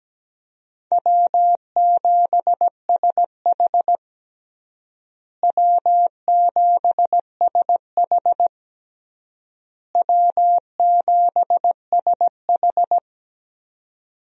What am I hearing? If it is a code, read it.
Morse code, heard as a keyed tone, about 17 wpm, 700 Hz. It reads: W7SH W7SH W7SH
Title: USA Call Signs - 4 Characters - Repeated 3x - 17wpm